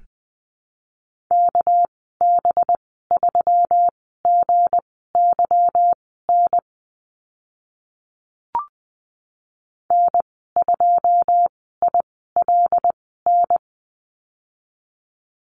Morse code, heard as a keyed tone, about 20 wpm, spaced as for 14 wpm, 700 Hz. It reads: KB3GYN N2ILN
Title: 207 Lesson 33g - Letter j - Call Signs 20wpm